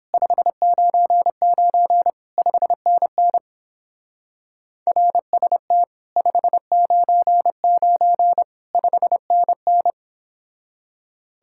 Morse code, heard as a keyed tone, about 29 wpm, 700 Hz. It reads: 599 5NN RST 599 5NN